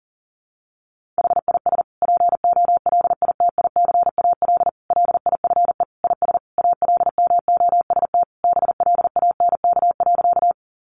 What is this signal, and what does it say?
Morse code, heard as a keyed tone, about 40 wpm, 700 Hz. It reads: HIS POLITICAL LIFE IS ALMOST BLANK.
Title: Sentences from Top 1500 Words 40wpm